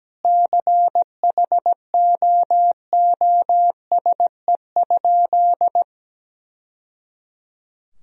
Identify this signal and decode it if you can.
Morse code, heard as a keyed tone, about 17 wpm, 700 Hz. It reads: CHOOSE?
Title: Sentences from Top 1000 Words - Review Part 4 - 17wpm